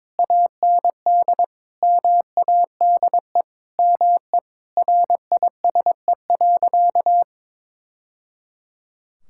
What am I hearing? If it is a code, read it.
Morse code, heard as a keyed tone, about 22 wpm, 700 Hz. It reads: AND MADE ME RISE.
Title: Sentences from Top 1000 Words - Review Part 6 - 22wpm